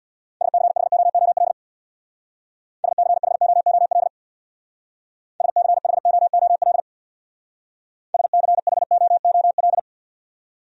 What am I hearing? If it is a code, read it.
Morse code, heard as a keyed tone, about 66 wpm, 700 Hz. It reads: SCHOOL SCHOOL SCHOOL SCHOOL